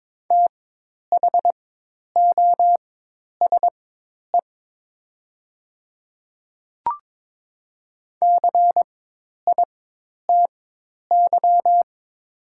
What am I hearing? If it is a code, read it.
Morse code, heard as a keyed tone, about 22 wpm, spaced as for 10 wpm, 700 Hz. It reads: THOSE CITY
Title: 4x Character Spacing - Top 300 Words - 22wpm